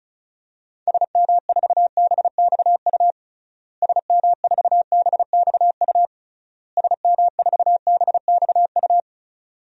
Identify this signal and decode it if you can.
Morse code, heard as a keyed tone, about 34 wpm, 700 Hz. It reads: SM4BXU SM4BXU SM4BXU
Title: DXCC-100 Call Signs - 6 Characters - Repeated 3x - 35wpm